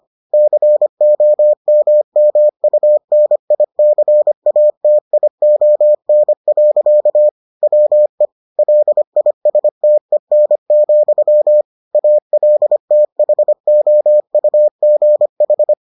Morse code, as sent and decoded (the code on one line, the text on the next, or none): -.-. --- -- -- ..- -. .. -.-. .- - .. --- -. .-.-.- / .-- . / .-.. .. ... - . -. --..-- / .- .-.. - .... --- ..- --. ....
COMMUNICATION. WE LISTEN, ALTHOUGH